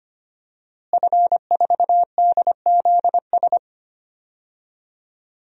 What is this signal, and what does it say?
Morse code, heard as a keyed tone, about 25 wpm, 700 Hz. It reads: F4DZS
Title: DXCC-100 Call Signs - 5 Characters - Repeated 3x - 25wpm